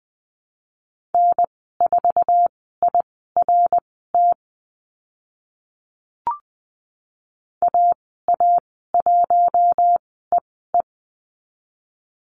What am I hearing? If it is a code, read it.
Morse code, heard as a keyed tone, about 20 wpm, spaced as for 14 wpm, 700 Hz. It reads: N4IRT AA1EE
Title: 082 Lesson 14g - Number 2 - Call Signs 20wpm